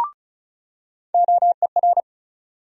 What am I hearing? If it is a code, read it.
Morse code, heard as a keyed tone, about 36 wpm, 700 Hz. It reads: OER
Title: USA Call Sign Postfixes - 3 Letters 35wpm